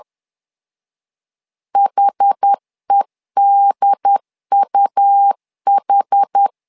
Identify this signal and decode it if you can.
Morse code, heard as a keyed tone, about 10 wpm, 800 Hz. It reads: HEDUH
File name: LEZIONI / 2 UHDN / 2 UHD 10wpm 1826